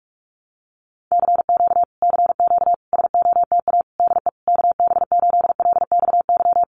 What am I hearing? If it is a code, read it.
Morse code, heard as a keyed tone, about 44 wpm, 700 Hz. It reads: CQ CQ SOTA DE KB8LXY
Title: Common QSO Phrases Encore 45wpm